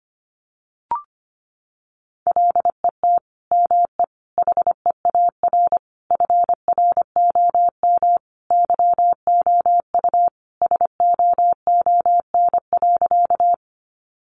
Morse code, heard as a keyed tone, about 25 wpm, 700 Hz. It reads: LET ME HEAR FROM YOU SOON.
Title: Sentences from Top 400 Words 25wpm